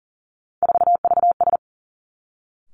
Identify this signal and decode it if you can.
Morse code, heard as a keyed tone, about 38 wpm, 690 Hz. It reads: 4VS
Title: Three Challenging Letter-Number Combinations 40wpm